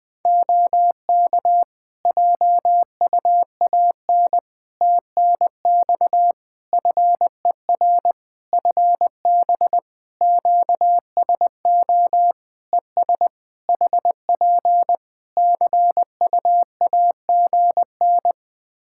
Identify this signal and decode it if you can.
Morse code, heard as a keyed tone, about 20 wpm, 700 Hz. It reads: OK JUAN TNX FER FB QSO ES HP CUAGN
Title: Practice QSO - Exchange 4 - Ending - 20wpm